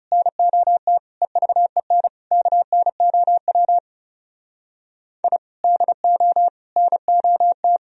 Morse code, heard as a keyed tone, about 39 wpm, 700 Hz. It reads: NOT EVEN KNOW I DO NOT